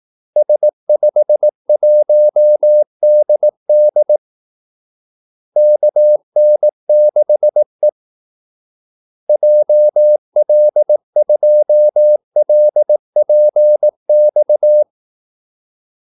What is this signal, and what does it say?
Morse code, heard as a keyed tone, about 18 wpm, 600 Hz. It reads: S51DD KN6E JL2LPX
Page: Kaldesignaler 18 wpm